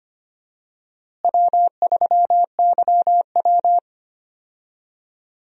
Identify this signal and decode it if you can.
Morse code, heard as a keyed tone, about 25 wpm, 700 Hz. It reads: W3YW
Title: USA Call Signs - 4 Characters - Repeated 3x - 25wpm